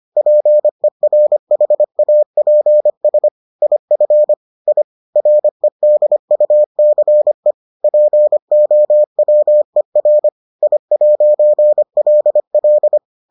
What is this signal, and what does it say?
Morse code, heard as a keyed tone, about 25 wpm, 600 Hz. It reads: PERHAPS IF I REDUCE POWER I'LL